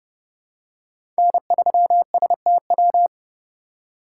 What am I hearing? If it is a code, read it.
Morse code, heard as a keyed tone, about 30 wpm, 700 Hz. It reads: N3STW
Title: USA Call Signs - 5 Characters - 2x Character Spacing - 30wpm